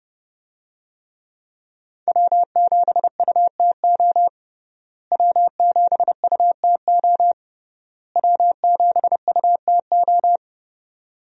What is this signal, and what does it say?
Morse code, heard as a keyed tone, about 30 wpm, 700 Hz. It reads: W7UTO W7UTO W7UTO
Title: USA Call Signs - 5 Characters - Repeated 3x - 30wpm